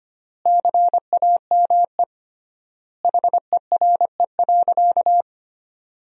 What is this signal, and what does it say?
Morse code, heard as a keyed tone, about 25 wpm, 700 Hz. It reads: CAME HERE.
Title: 3x Word-Spacing - Sentences from Top 500 Words - 25wpm